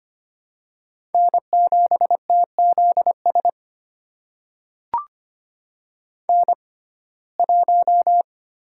Morse code, heard as a keyed tone, about 25 wpm, spaced as for 18 wpm, 700 Hz. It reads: N7TZS N1
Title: USA Call Signs - 5 Characters - 6x Character Spacing - 25wpm